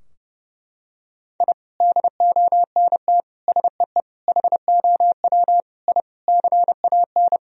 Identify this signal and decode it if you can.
Morse code, heard as a keyed tone, about 30 wpm, 700 Hz. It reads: I DONT SEE HOW I CAN
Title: Sentences from Top 1000 Words - Review Part 2 - 30wpm